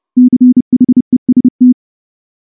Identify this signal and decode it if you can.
Morse code, heard as a keyed tone, about 30 wpm, 250 Hz. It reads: CHEST
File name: morse / morse-training-30WPM